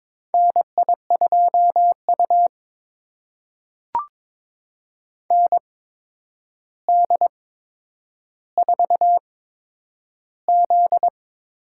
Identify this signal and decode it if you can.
Morse code, heard as a keyed tone, about 22 wpm, spaced as for 9 wpm, 700 Hz. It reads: NI2U ND4Z
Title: USA Call Signs - 4 Characters - 8x Character Spacing - 22wpm